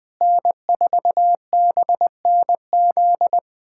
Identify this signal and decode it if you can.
Morse code, heard as a keyed tone, about 20 wpm, 700 Hz. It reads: N4BNZ
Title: USA Callsign DE Callsign - 5 Characters - 20wpm